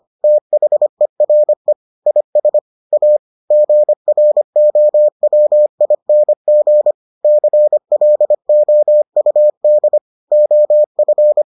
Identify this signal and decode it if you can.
Morse code, heard as a keyed tone, about 25 wpm, 600 Hz. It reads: THERE IS A GROWING CLOUD OF